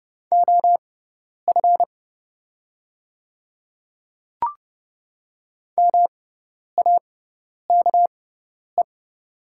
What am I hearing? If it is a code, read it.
Morse code, heard as a keyed tone, about 30 wpm, spaced as for 10 wpm, 700 Hz. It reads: OF MAKE
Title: 6x Character Spacing - Top 100 Words - 30wpm